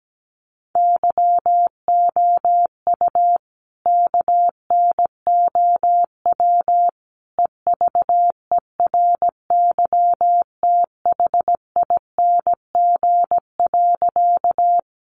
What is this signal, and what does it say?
Morse code, heard as a keyed tone, about 17 wpm, 700 Hz. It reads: YOU KNOW EVERYTHING.
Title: Sentences from Top 700 Words 17wpm